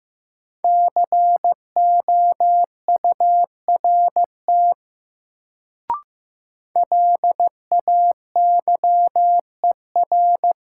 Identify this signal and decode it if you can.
Morse code, heard as a keyed tone, about 15 wpm, 700 Hz. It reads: COURT LAYER
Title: Sentences from Top 1600 Words - New Word Drill Alt - 15wpm